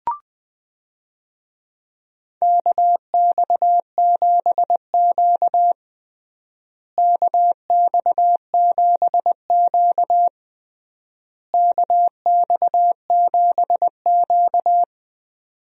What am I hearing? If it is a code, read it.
Morse code, heard as a keyed tone, about 20 wpm, 700 Hz. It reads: KX7Q KX7Q KX7Q
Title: USA Call Signs - 4 Characters - Repeated 3x - 20wpm